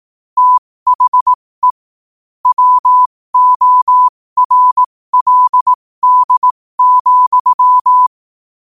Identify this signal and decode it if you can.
Morse code, heard as a keyed tone, about 18 wpm, spaced as for 15 wpm, 1000 Hz. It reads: THE WORLD,
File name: qotdfiles / 15wpm